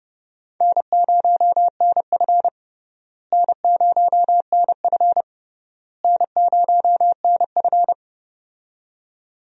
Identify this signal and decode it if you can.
Morse code, heard as a keyed tone, about 32 wpm, 700 Hz. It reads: N0NF N0NF N0NF
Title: USA Call Signs - 4 Characters - Repeated 3x - 30wpm